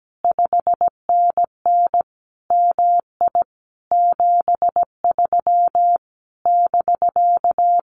Morse code, heard as a keyed tone, about 17 wpm, 700 Hz. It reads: 5NN MI 73 <BK>
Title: Common QSO Phrases Encore 17wpm